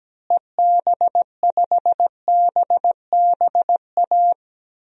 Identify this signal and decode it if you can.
Morse code, heard as a keyed tone, about 17 wpm, 700 Hz. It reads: EB5BBA
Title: DXCC-100 Call Signs - 6 Characters - Repeated 3x - 17wpm